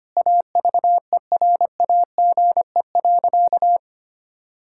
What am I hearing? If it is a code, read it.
Morse code, heard as a keyed tone, about 25 wpm, 700 Hz. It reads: AVERAGE.
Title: Sentences from Top 1000 Words - Review Part 6 - 25wpm